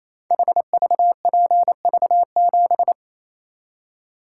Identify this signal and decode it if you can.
Morse code, heard as a keyed tone, about 27 wpm, 700 Hz. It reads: HVPV7
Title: Code Groups - 5 Letter:Numbers - 28wpm